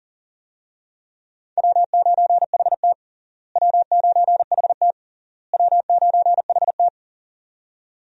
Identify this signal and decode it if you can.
Morse code, heard as a keyed tone, about 37 wpm, 700 Hz. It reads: W9HT W9HT W9HT
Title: USA Call Signs - 4 Characters - Repeated 3x - 40wpm